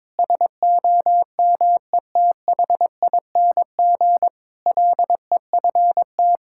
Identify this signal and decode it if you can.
Morse code, heard as a keyed tone, about 22 wpm, 700 Hz. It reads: SOMETHING LEFT
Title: Sentences from Top 1000 Words - Review Part 6 - 22wpm